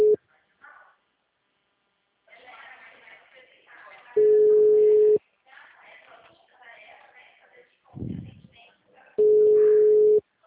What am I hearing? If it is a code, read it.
Ringback tone (Europe, 425 Hz), with speech.